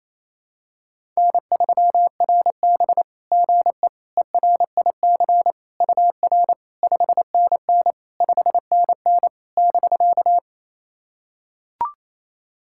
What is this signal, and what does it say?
Morse code, heard as a keyed tone, about 27 wpm, 690 Hz. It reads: N3RB GE ERIC UR 5NN 5NN <BK>
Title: SOTA QSO Practice 28wpm